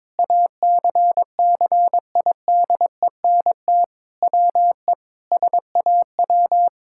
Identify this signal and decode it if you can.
Morse code, heard as a keyed tone, about 22 wpm, 700 Hz. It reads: ACCIDENT WE SAW
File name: practice / 22wpm / Sentences with 4-Word Phrases - 22wpm